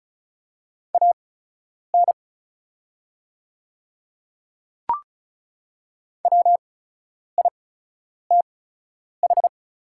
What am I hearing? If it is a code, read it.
Morse code, heard as a keyed tone, about 36 wpm, spaced as for 10 wpm, 700 Hz. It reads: AN WITH
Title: 8x Character Spacing - Top 100 Words - 35wpm